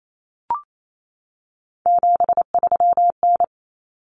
Morse code, heard as a keyed tone, about 28 wpm, 700 Hz. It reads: 73N
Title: Common Cut Numbers - 3-Digit 28wpm